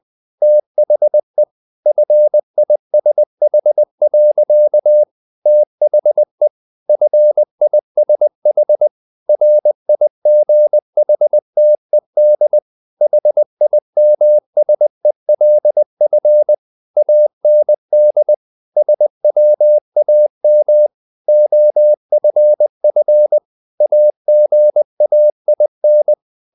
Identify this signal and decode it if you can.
Morse code, heard as a keyed tone, about 20 wpm, 600 Hz. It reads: THE FISH. THE FISH RIGHTED HIMSELF AND SWAM OFF AGAIN
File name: the old man and the sea part 2 20wpm